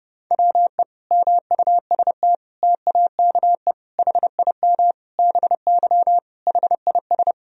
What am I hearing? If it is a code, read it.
Morse code, heard as a keyed tone, about 30 wpm, 700 Hz. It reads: WE MUST TAKE HIM BY HIS
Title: Sentences from Top 1000 Words - Review Part 1 - 30wpm